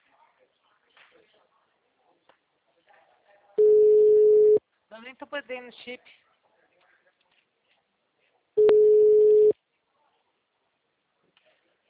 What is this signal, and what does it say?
Ringback tone (Europe, 425 Hz), with speech.